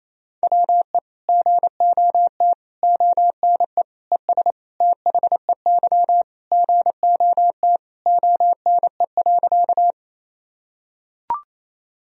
Morse code, heard as a keyed tone, about 28 wpm, 700 Hz. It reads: WE GOT ONE ES THEY GOT ONE.
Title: Sentences from Top 300 Words and Abbreviations 28wpm